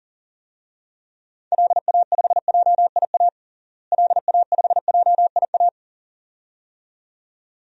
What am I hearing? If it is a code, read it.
Morse code, heard as a keyed tone, about 40 wpm, 700 Hz. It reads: LA5JIA LA5JIA
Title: DXCC-100 Call Signs - 6 Characters - Repeated 2x - 40wpm